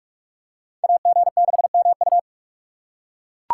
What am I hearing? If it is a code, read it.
Morse code, heard as a keyed tone, about 41 wpm, 700 Hz. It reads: AG6MU
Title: USA Call Signs - 2x Character Spacing - 45wpm